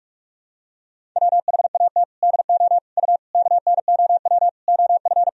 Morse code, heard as a keyed tone, about 46 wpm, 700 Hz. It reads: WHAT DO U KNOW OF
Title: Sentences from Top 100 Words and Abbreviations 45wpm